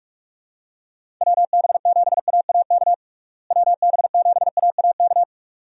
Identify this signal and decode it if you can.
Morse code, heard as a keyed tone, about 46 wpm, 700 Hz. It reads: WB7AAK WB7AAK
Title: USA Call Signs - 6 Characters - Repeated 2x - 45wpm